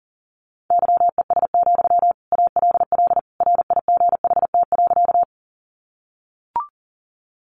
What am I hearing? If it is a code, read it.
Morse code, heard as a keyed tone, about 40 wpm, 700 Hz. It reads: YES, ALL RIGHT.
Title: Sentences from Top 500 Words 40wpm